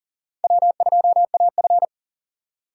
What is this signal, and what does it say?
Morse code, heard as a keyed tone, about 38 wpm, 700 Hz. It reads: W2AF
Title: USA Call Signs - 4 Characters - 2x Character Spacing - 40wpm